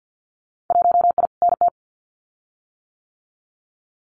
Morse code, heard as a keyed tone, about 50 wpm, 700 Hz. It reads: 1I NT
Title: Winter Field Day Exchanges Encore 50wpm